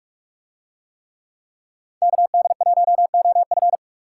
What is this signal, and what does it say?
Morse code, heard as a keyed tone, about 40 wpm, 700 Hz. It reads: KD1OF